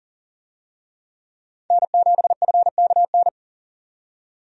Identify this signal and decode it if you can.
Morse code, heard as a keyed tone, about 40 wpm, 700 Hz. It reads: N7FKN
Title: USA Call Signs - 5 Characters - 40wpm